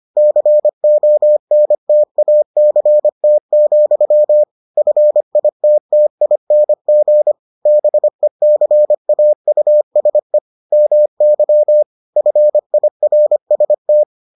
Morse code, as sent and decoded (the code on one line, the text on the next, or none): -.-. --- -. - .- -.-. - --..-- / ..-. .. - - .. -. --. / -... -.-. .- ..- ... / -- -.-- / ..-. .. .-. ... -
CONTACT, FITTING BECAUSE MY FIRST